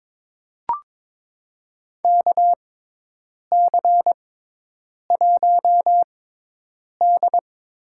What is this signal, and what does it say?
Morse code, heard as a keyed tone, about 23 wpm, spaced as for 8 wpm, 700 Hz. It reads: KC1D
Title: USA Call Signs - 6x Character Spacing - 22wpm